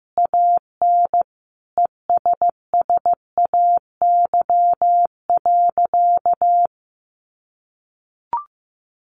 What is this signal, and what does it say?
Morse code, heard as a keyed tone, about 15 wpm, 700 Hz. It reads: AN ESSAY.
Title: Sentences from Top 1500 Words 15wpm